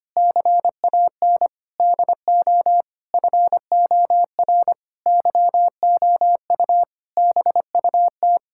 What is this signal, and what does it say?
Morse code, heard as a keyed tone, about 25 wpm, 700 Hz. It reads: CAN DO FOR YOU BUT
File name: practice / 25wpm / Sentences with 9-Word Phrases - 25wpm